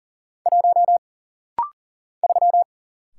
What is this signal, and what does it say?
Morse code, heard as a keyed tone, about 40 wpm, 700 Hz. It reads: 1 3
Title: Single Letter-Number Alt - Rapid-Fire 40wpm